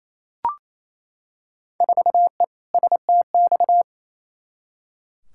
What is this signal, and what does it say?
Morse code, heard as a keyed tone, about 28 wpm, 700 Hz. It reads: 4E STX
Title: Field Day Exchanges Encore - 28wpm